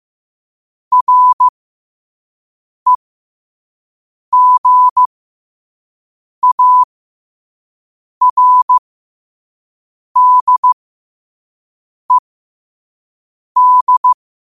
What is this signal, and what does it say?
Morse code, heard as a keyed tone, about 15 wpm, spaced as for 5 wpm, 1000 Hz. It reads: REGARDED
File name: qotdfiles / 5wpm